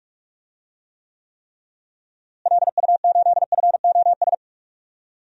Morse code, heard as a keyed tone, about 46 wpm, 700 Hz. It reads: LU8FOS
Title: DXCC-100 Call Signs - 6 Characters - Repeated 3x - 45wpm